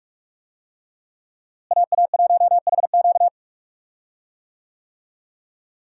Morse code, heard as a keyed tone, about 42 wpm, 700 Hz. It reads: AA1HQ
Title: USA Call Signs - 5 Characters - Repeated 2x - 45wpm